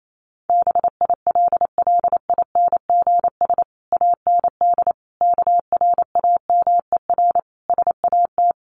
Morse code, heard as a keyed tone, about 28 wpm, 700 Hz. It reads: BILLINGS AND KRAMER SAT
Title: A Sound Of Thunder 28wpm